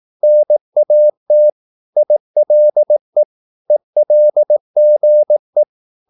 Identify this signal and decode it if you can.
Morse code, heard as a keyed tone, about 18 wpm, 600 Hz. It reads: NAT ILE ELGE
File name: Korte ord DK 18wpm